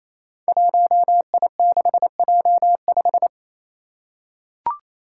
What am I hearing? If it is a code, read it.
Morse code, heard as a keyed tone, about 28 wpm, 700 Hz. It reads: 1I6J5
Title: Five Challenging Letter-Number Combinations 28wpm